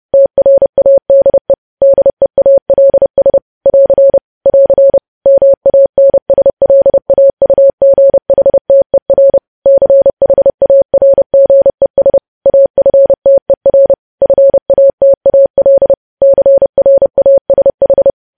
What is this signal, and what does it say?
Morse code, heard as a keyed tone, about 30 wpm, 550 Hz. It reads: TRADE DEALS <AR> <AR> MANSLAUGHTER CHARGES AFTER FATAL CRASH
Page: News Headlines in Morse Code at 30 WPM